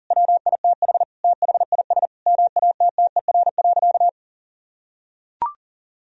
Morse code, heard as a keyed tone, about 40 wpm, 700 Hz. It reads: WITH THIS MATTER.
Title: Sentences from Top 1000 Words - Review Part 2 - 40wpm